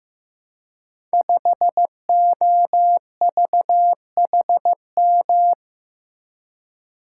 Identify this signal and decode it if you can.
Morse code, heard as a keyed tone, about 15 wpm, 700 Hz. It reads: 5OVHM
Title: Code Groups - 5 Letter:Numbers - 15wpm